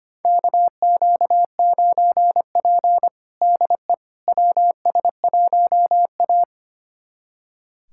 Morse code, heard as a keyed tone, about 25 wpm, 700 Hz. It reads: KQ9P DE WS1A
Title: USA Callsign DE Callsign - 4 Characters - 25wpm